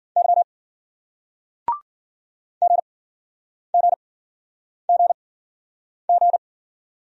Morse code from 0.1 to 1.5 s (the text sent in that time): Y